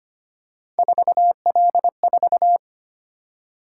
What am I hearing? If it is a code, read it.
Morse code, heard as a keyed tone, about 25 wpm, 700 Hz. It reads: 4L4
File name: Three Letter-Number Combinations - 25wpm